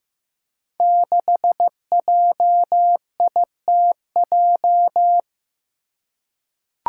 Morse code, heard as a keyed tone, about 15 wpm, 700 Hz. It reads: 6JITJ
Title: Five Challenging Letter-Number Combinations 15wpm